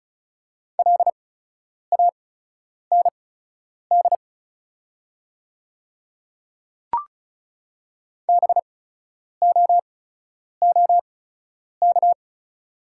Morse code, heard as a keyed tone, about 36 wpm, spaced as for 10 wpm, 700 Hz. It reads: LAND BOOK
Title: 8x Character Spacing - Top 500 Words - 35wpm